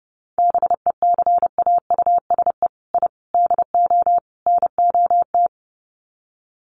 Morse code, heard as a keyed tone, about 30 wpm, 700 Hz. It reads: BECAUSE I DO NOT
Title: Sets of 4 Words 30wpm